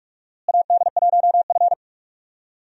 Morse code, heard as a keyed tone, about 44 wpm, 700 Hz. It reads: AD1F
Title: USA Call Signs - 4 Characters - Repeated 2x - 45wpm